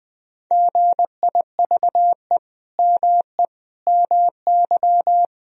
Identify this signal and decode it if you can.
Morse code, heard as a keyed tone, about 20 wpm, 700 Hz. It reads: GIVE ME MY
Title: Sentences from Top 1000 Words - Review Part 3 - 20wpm